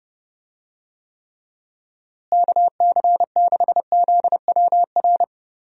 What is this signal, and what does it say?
Morse code, heard as a keyed tone, about 30 wpm, 700 Hz. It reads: KC6ZWR
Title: USA Call Signs - 6 Characters - Repeated 3x - 30wpm